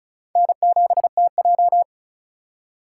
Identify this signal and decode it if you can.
Morse code, heard as a keyed tone, about 34 wpm, 700 Hz. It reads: N7TJ
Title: USA Call Signs - 4 Characters - 3x Character Spacing - 35wpm